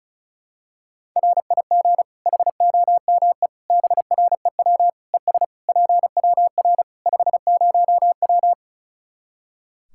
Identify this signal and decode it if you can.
Morse code, heard as a keyed tone, about 36 wpm, 700 Hz. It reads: RIG HOME BREW ES PWR 50W
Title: Practice QSO - Exchange 2 - Info - 35wpm